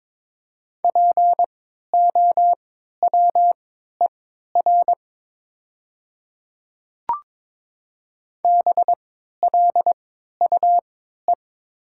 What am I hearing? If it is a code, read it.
Morse code, heard as a keyed tone, about 22 wpm, spaced as for 13 wpm, 700 Hz. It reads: POWER BLUE
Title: 3x Character Spacing - Top 500 Words - 22wpm